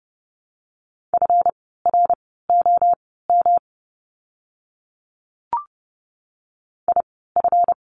Morse code, heard as a keyed tone, about 30 wpm, spaced as for 17 wpm, 700 Hz. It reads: FROM IF
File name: course / 30:10wpm / 237 Lesson 38f - Letter q - Words 30wpm